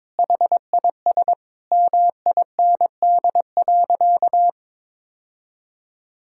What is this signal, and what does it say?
Morse code, heard as a keyed tone, about 22 wpm, 700 Hz. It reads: HIS MIND.
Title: Sentences from Top 1000 Words - Review Part 5 - 22wpm